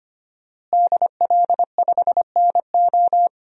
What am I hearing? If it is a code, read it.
Morse code, heard as a keyed tone, about 25 wpm, 700 Hz. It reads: DL5NO
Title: DXCC-100 Call Signs - 5 Characters - Repeated 3x - 25wpm